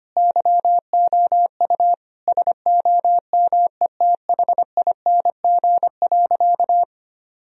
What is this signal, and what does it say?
Morse code, heard as a keyed tone, about 25 wpm, 700 Hz. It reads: YOU SOMETHING.
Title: Sentences from Top 1000 Words - Review Part 3 - 25wpm